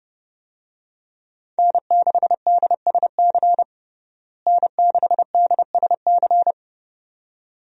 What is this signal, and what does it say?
Morse code, heard as a keyed tone, about 30 wpm, 700 Hz. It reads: N6DSC N6DSC
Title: USA Call Signs - 5 Characters - Repeated 2x - 30wpm